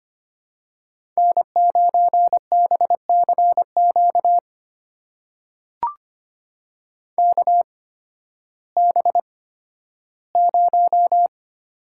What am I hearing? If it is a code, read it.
Morse code, heard as a keyed tone, about 25 wpm, spaced as for 13 wpm, 700 Hz. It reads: N9BCQ KB0
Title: USA Call Signs - 8x Character Spacing - 25wpm